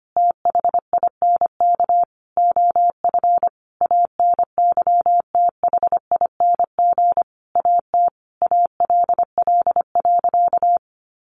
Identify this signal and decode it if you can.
Morse code, heard as a keyed tone, about 25 wpm, 700 Hz. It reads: THINK OF ANYTHING AT ALL.
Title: Sentences from Top 1300 Words 25wpm